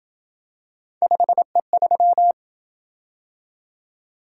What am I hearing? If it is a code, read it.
Morse code, heard as a keyed tone, about 27 wpm, 700 Hz. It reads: 5E3